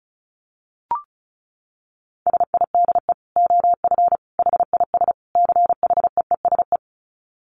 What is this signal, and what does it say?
Morse code, heard as a keyed tone, about 34 wpm, 700 Hz. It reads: SIDE OF HIS CHEESE
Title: Dit Challenge 35wpm